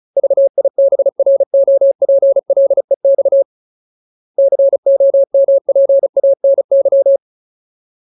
Morse code, heard as a keyed tone, about 34 wpm, 550 Hz. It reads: VIBROPLEX COMPANY
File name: Ultimate Word List / 35WPM Ultimate